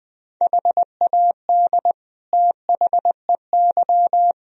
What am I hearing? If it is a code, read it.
Morse code, heard as a keyed tone, about 19 wpm, 700 Hz. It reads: HAD THEY
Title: Sentences from Top 1000 Words - Review Part 2 - 20wpm